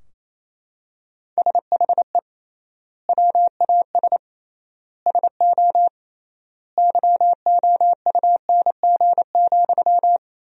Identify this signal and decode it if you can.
Morse code, heard as a keyed tone, about 28 wpm, 700 Hz. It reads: SHE WAS SO YOUNG,
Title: 3x Word-Spacing - Sentences from Top 400 Words - 28wpm